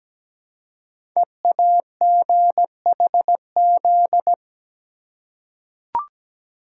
Morse code, heard as a keyed tone, about 17 wpm, 700 Hz. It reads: EAGHZ
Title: Code Groups - 5 Letters - 17wpm